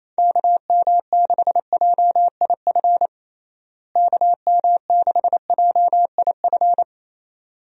Morse code, heard as a keyed tone, about 28 wpm, 700 Hz. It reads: KM6JIF KM6JIF
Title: USA Call Signs - 6 Characters - Repeated 2x - 28wpm